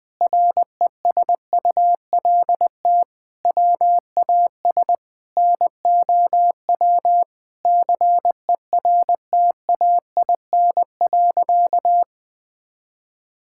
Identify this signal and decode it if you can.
Morse code, heard as a keyed tone, about 20 wpm, 700 Hz. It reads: RESULT WAS NOW CERTAIN.
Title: Sentences from Top 1000 Words - Review Part 5 - 20wpm